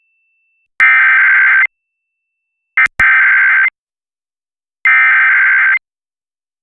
Emergency Alert System (SAME) tones (header bursts), heard as harsh buzzing bursts. It is heard from 0.8 to 5.7 s.